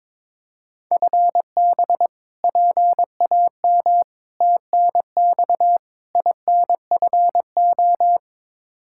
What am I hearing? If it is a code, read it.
Morse code, heard as a keyed tone, about 22 wpm, 700 Hz. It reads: FB PAM TNX INFO
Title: K1USN SST QSO Practice 22wpm